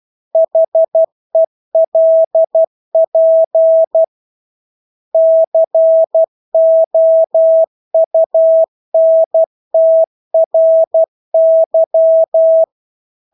Morse code, heard as a keyed tone, about 12 wpm, 650 Hz. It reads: HELP COUNTRY
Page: Engelske ord 12 wpm